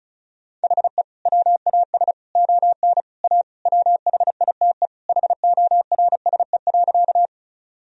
Morse code, heard as a keyed tone, about 36 wpm, 700 Hz. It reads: HE WAS ON A WHITE HORSE.
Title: Sentences from Top 1000 Words - Review Part 1 - 35wpm